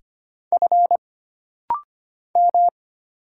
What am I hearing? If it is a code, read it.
Morse code, heard as a keyed tone, about 25 wpm, 700 Hz. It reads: F M
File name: Single Character (Letter:Number:Punctuation:Prosign) - Rapid-Fire - 25wpm